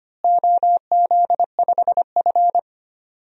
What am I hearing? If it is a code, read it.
Morse code, heard as a keyed tone, about 25 wpm, 700 Hz. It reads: OZ5F
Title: DXCC-100 Call Signs - 4 Characters - Repeated 3x - 25wpm